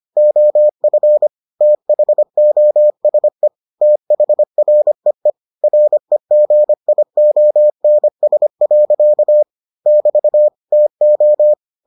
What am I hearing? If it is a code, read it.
Morse code, heard as a keyed tone, about 25 wpm, 600 Hz. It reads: OF THOSE THREE REGIONS. <BT> TO